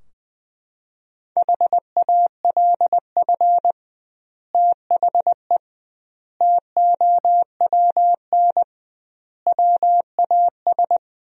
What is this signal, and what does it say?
Morse code, heard as a keyed tone, about 20 wpm, 700 Hz. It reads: HALF THE TOWN WAS
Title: 2x Word-Spacing - Sentences from Top 500 Words - 20wpm